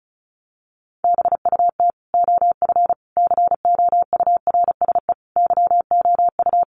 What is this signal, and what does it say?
Morse code, heard as a keyed tone, about 34 wpm, 700 Hz. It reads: BUT OF COURSE YOU
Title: Sentences from Top 1500 Words 35wpm